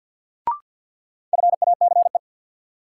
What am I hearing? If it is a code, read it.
Morse code, heard as a keyed tone, about 53 wpm, 700 Hz. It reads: FACE
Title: Sentences from Top 300 Words - New Word Drill - 50wpm